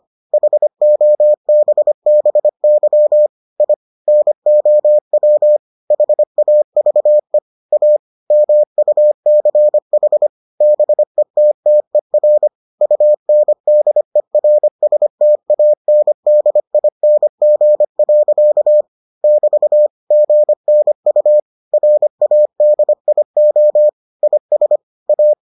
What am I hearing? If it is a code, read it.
Morse code, heard as a keyed tone, about 25 wpm, 600 Hz. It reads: HOBBY I NOW HAVE A MUCH BETTER UNDERSTANDING. <BT> GNU RADIO IS A